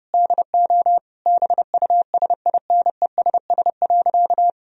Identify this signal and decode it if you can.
Morse code, heard as a keyed tone, about 30 wpm, 700 Hz. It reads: DO BUSINESS.
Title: Sentences from Top 1000 Words - Review Part 3 - 30wpm